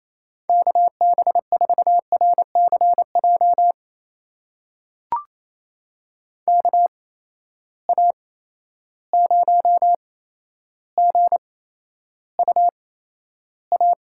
Morse code, heard as a keyed tone, about 28 wpm, spaced as for 12 wpm, 700 Hz. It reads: KB4RCJ KA0GUA
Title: USA Call Signs - 6 Characters - 8x Character Spacing - 28wpm